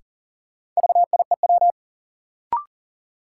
Morse code, heard as a keyed tone, about 40 wpm, 700 Hz. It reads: VIEW
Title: Sets of 5 Words - New Word Drill - 40wpm